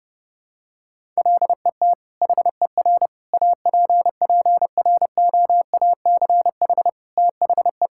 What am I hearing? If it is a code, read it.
Morse code, heard as a keyed tone, about 32 wpm, 700 Hz. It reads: LET HER APPROACH THE
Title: Sentences from Top 1000 Words - Review Part 6 - 30wpm